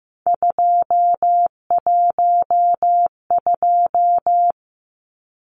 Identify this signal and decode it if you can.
Morse code, heard as a keyed tone, about 15 wpm, 700 Hz. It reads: 212
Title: Numbers - 3-Digit 15wpm